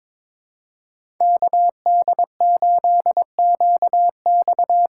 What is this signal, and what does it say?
Morse code, heard as a keyed tone, about 22 wpm, 700 Hz. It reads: KD8QX
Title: USA Call Signs - 6 Characters - 6x Character Spacing - 22wpm